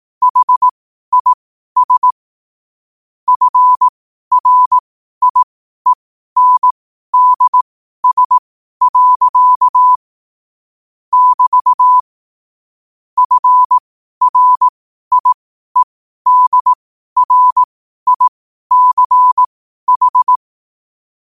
Morse code, heard as a keyed tone, about 18 wpm, spaced as for 12 wpm, 1000 Hz. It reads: HIS FRIENDS. <BT> FRIEDRICH
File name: qotdfiles / 12wpm